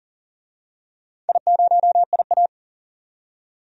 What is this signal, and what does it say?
Morse code, heard as a keyed tone, about 40 wpm, 700 Hz. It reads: I0IA
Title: DXCC-100 Call Signs - 4 Characters - 40wpm